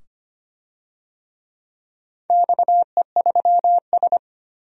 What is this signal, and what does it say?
Morse code, heard as a keyed tone, about 25 wpm, 700 Hz. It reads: XE3S
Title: DXCC-100 Call Signs - 4 Characters - 25wpm